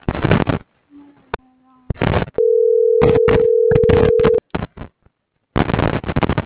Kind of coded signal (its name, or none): telephone call progress tone